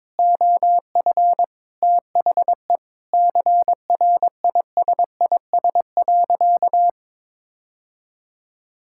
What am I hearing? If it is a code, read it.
Morse code, heard as a keyed tone, about 22 wpm, 700 Hz. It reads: OF THE CRISIS.
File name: practice / 22wpm / Sentences with 8-Word Phrases - 22wpm